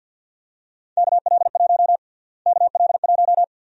Morse code, heard as a keyed tone, about 49 wpm, 700 Hz. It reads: KL1 KL1
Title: USA Call Sign Prefixes - Repeated 2x - 50wpm